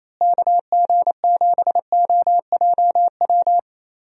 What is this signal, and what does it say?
Morse code, heard as a keyed tone, about 28 wpm, 700 Hz. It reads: KG7OJW
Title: USA Call Signs - 6 Characters - Repeated 2x - 28wpm